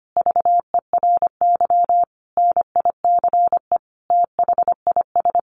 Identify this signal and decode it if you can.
Morse code, heard as a keyed tone, about 25 wpm, 700 Hz. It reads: VERY NICE THIS
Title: Sentences from Top 1500 Words 25wpm